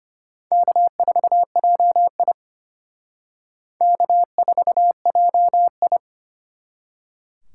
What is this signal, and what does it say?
Morse code, heard as a keyed tone, about 31 wpm, 700 Hz. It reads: K4JI K4JI
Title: Speed Racing - USA 1x2 Call Signs 25wpm